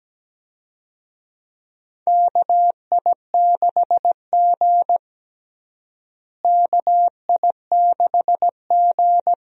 Morse code, heard as a keyed tone, about 17 wpm, 700 Hz. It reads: KI6G KI6G
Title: USA Call Signs - 4 Characters - Repeated 3x - 17wpm